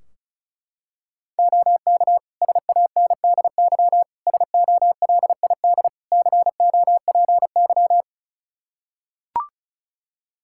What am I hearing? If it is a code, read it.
Morse code, heard as a keyed tone, about 36 wpm, 700 Hz. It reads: OK SANDY SOLID COPY
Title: Practice QSO - Exchange 3 - More Info - 35wpm